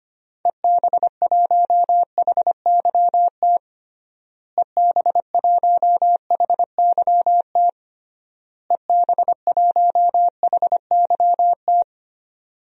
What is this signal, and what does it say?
Morse code, heard as a keyed tone, about 25 wpm, 700 Hz. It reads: EB1HYT EB1HYT EB1HYT
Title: DXCC-100 Call Signs - 6 Characters - Repeated 3x - 25wpm